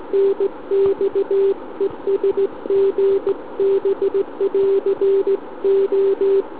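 Morse code, heard as a keyed tone, about 18 wpm, 390 Hz. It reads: NX ES GB <AR> O
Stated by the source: SM7ALC heard via tropo on 144 MHz with an FT817 and a quarter-wave antenna on a central-heating radiator.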